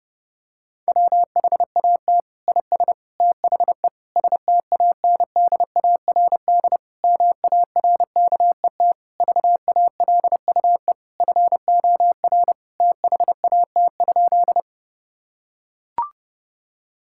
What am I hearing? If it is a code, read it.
Morse code, heard as a keyed tone, about 30 wpm, 700 Hz. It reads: WHAT IS THE STANDARD MARKET VALUE FOR THAT?
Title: Sentences from Top 1000 Words - Review Part 6 - 30wpm